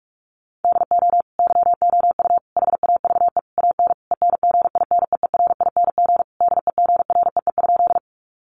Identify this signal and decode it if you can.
Morse code, heard as a keyed tone, about 44 wpm, 700 Hz. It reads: DO YOU HAVE AN ENGINEERING DEGREE?
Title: Sentences from Top 1500 Words 45wpm